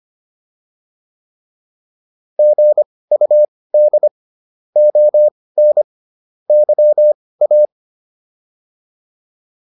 Morse code, heard as a keyed tone, about 25 wpm, spaced as for 18 wpm, 600 Hz. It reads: GUD ON YA